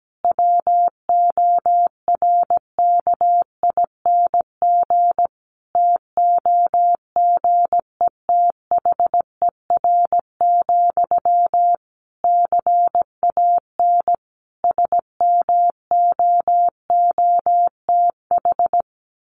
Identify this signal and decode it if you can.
Morse code, heard as a keyed tone, about 17 wpm, 700 Hz. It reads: WORKING TOGETHER, CAN SMOOTH
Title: Sentences from Top 1400 Words 17wpm